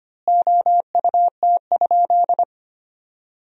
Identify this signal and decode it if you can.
Morse code, heard as a keyed tone, about 25 wpm, 700 Hz. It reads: OUT?
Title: 2x Word-Spacing - Sentences from Top 100 Words - 25wpm